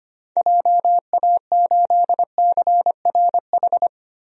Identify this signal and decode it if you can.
Morse code, heard as a keyed tone, about 25 wpm, 700 Hz. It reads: JA8CRH
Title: DXCC-100 Call Signs - 6 Characters - Repeated 3x - 25wpm